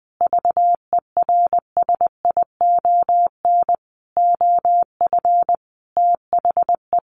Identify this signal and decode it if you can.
Morse code, heard as a keyed tone, about 20 wpm, 700 Hz. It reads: VERSION OF THE
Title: Sentences from Top 1400 Words 20wpm